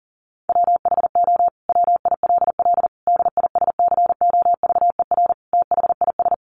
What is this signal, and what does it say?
Morse code, heard as a keyed tone, about 40 wpm, 700 Hz. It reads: WHO WILL DISCOVER THIS
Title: Sentences from Top 1100 Words 40wpm